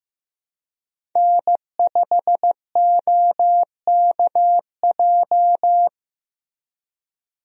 Morse code, heard as a keyed tone, about 15 wpm, 700 Hz. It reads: N5OKJ
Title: USA Call Signs - 5 Characters - Repeated 2x - 15wpm